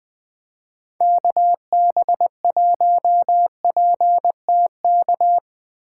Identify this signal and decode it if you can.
Morse code, heard as a keyed tone, about 23 wpm, 700 Hz. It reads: KB1PTK
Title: Speed Racing - USA 2x3 Call Signs 20wpm